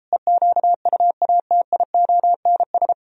Morse code, heard as a keyed tone, about 38 wpm, 700 Hz. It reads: EQUATIONS